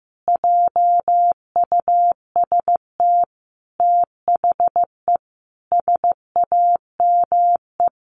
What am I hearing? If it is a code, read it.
Morse code, heard as a keyed tone, about 14 wpm, 700 Hz. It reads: JUST THE SAME
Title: Sentences from Top 800 Words 15wpm